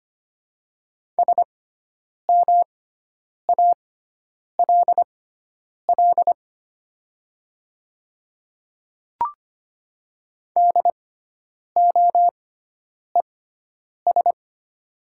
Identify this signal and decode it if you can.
Morse code, heard as a keyed tone, about 25 wpm, spaced as for 9 wpm, 690 Hz. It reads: SMALL DOES
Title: 6x Character Spacing - Top 200 Words - 25wpm